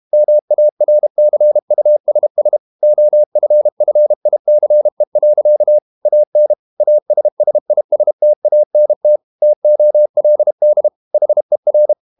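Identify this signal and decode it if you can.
Morse code, heard as a keyed tone, about 32 wpm, 600 Hz. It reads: MARCUSS OFFICE. AN ASSISTANT TOLD HER
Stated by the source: Morse code practice at 32 wpm from CW med Gnister, reading Ken Follett's novel Never in English.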